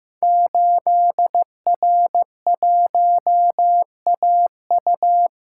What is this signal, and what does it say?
Morse code, heard as a keyed tone, about 15 wpm, 700 Hz. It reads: 8R1AU
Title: Code Groups - 5 Letter:Numbers - 15wpm